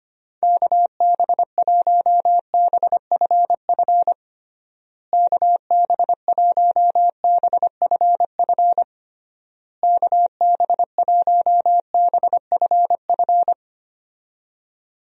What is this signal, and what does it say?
Morse code, heard as a keyed tone, about 25 wpm, 700 Hz. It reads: KB1BFF KB1BFF KB1BFF
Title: USA Call Signs - 6 Characters - Repeated 3x - 25wpm